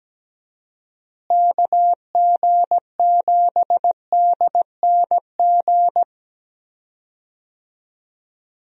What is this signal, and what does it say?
Morse code, heard as a keyed tone, about 17 wpm, 700 Hz. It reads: KG7DNG
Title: USA Call Signs - 6 Characters - Repeated 2x - 17wpm